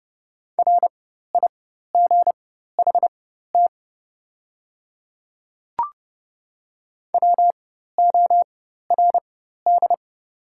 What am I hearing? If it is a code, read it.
Morse code, heard as a keyed tone, about 31 wpm, spaced as for 14 wpm, 700 Hz. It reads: RIGHT WORD
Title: 4x Character Spacing - Top 200 Words - 30wpm